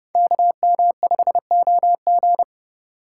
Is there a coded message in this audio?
Yes: Morse code from 0.2 to 2.4 s, heard as a keyed tone, about 29 wpm, 700 Hz. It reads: KM5OG